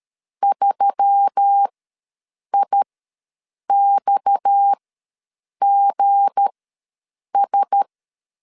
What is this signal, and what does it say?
Morse code, heard as a keyed tone, about 13 wpm, spaced as for 7 wpm, 800 Hz. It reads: 3IXGS
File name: LEZIONI / 7 GF9Z / 7 G ws 12wpm 3140